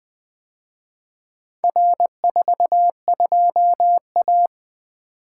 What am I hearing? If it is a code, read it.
Morse code, heard as a keyed tone, about 23 wpm, 700 Hz. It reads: R42A
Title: Speed-Racing - Four Letter-Number Combinations - 20wpm